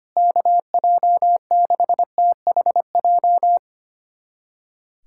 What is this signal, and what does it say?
Morse code, heard as a keyed tone, about 25 wpm, 700 Hz. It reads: KJ6THJ
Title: USA Callsign DE Callsign - 6 Characters - 25wpm